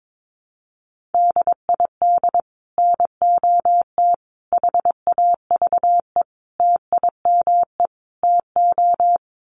Morse code, heard as a keyed tone, about 22 wpm, 700 Hz. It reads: DID NOT HAVE TIME TO
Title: Sets of 5 Words Encore 22wpm